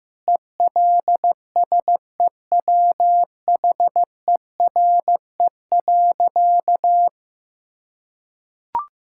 Morse code, heard as a keyed tone, about 15 wpm, 700 Hz. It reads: ELSEWHERE.
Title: Sentences from Top 1600 Words 15wpm